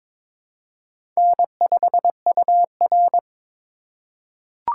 Morse code, heard as a keyed tone, about 22 wpm, 700 Hz. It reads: N5UR
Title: USA Call Signs - 4 Characters - 4x Character Spacing - 22wpm